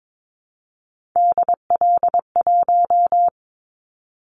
Morse code, heard as a keyed tone, about 22 wpm, 700 Hz. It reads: DL1
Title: Callsign Prefixes for Top 50 DXCC - 22wpm